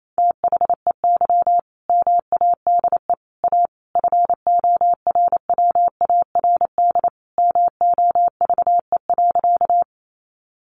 Morse code, heard as a keyed tone, about 28 wpm, 700 Hz. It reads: THEY MADE A FORWARD MOVE.
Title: Sentences from Top 1300 Words 28wpm